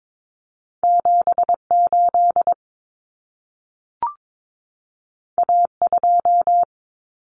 Morse code, heard as a keyed tone, about 22 wpm, 700 Hz. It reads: 78 A2
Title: Common Cut Numbers - 2-Digit 22wpm